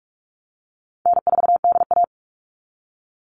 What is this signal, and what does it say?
Morse code, heard as a keyed tone, about 44 wpm, 690 Hz. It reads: N4DA
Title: NA SOTA Chaser Callsigns 45wpm